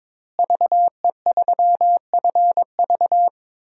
Morse code, heard as a keyed tone, about 22 wpm, 700 Hz. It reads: VE3FV
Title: DXCC-100 Call Signs - 5 Characters - Repeated 3x - 22wpm